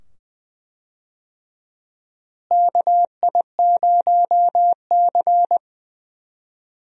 Morse code, heard as a keyed tone, about 21 wpm, 700 Hz. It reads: KI0C
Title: USA Call Signs - 4 Characters - Repeated 2x - 20wpm